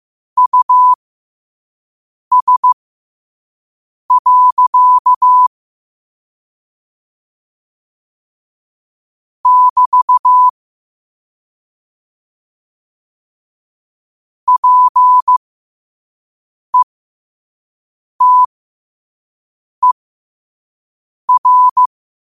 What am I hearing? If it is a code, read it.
Morse code, heard as a keyed tone, about 15 wpm, spaced as for 5 wpm, 1000 Hz. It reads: US. <BT> PETER
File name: qotdfiles / 5wpm